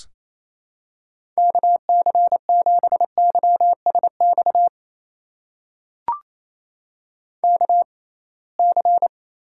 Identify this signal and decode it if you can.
Morse code, heard as a keyed tone, about 28 wpm, spaced as for 21 wpm, 700 Hz. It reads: KC7YSX KC
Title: USA Call Signs - 6 Characters - 6x Character Spacing - 28wpm